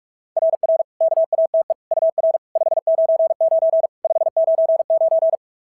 Morse code, heard as a keyed tone, about 42 wpm, 690 Hz. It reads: RR KATE UR 599 599
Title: SOTA QSO Practice 45wpm